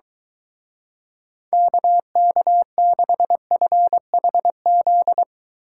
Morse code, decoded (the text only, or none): KK6FHZ